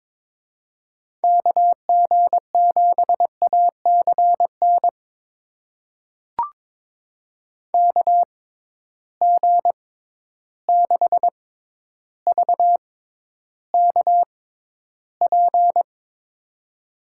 Morse code, heard as a keyed tone, about 22 wpm, spaced as for 11 wpm, 700 Hz. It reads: KG7ACN KG6VKP
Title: USA Call Signs - 6 Characters - 6x Character Spacing - 22wpm